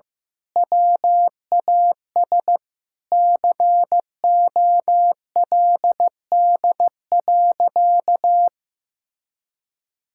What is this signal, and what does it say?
Morse code, heard as a keyed tone, about 14 wpm, 700 Hz. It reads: WAS COLD.
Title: Sentences from Top 1000 Words - Review Part 5 - 15wpm